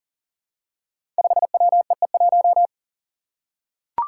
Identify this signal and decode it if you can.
Morse code, heard as a keyed tone, about 40 wpm, 700 Hz. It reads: 5WEE1
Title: Five Challenging Letter-Number Combinations 40wpm